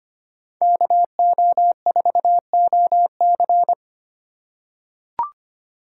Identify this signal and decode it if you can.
Morse code, heard as a keyed tone, about 25 wpm, 700 Hz. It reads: KO4OC
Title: USA Call Signs - 3x Character Spacing - 25wpm